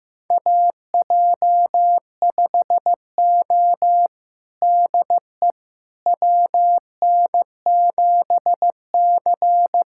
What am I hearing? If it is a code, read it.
Morse code, heard as a keyed tone, about 16 wpm, 700 Hz. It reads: AJ5O DE WN7C
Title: USA Callsign DE Callsign - 4 Characters - 15wpm